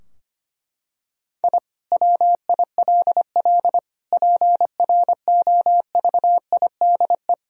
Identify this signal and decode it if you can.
Morse code, heard as a keyed tone, about 25 wpm, 700 Hz. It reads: I WILL PROVIDE
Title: Sentences from Top 1000 Words - Review Part 3 - 25wpm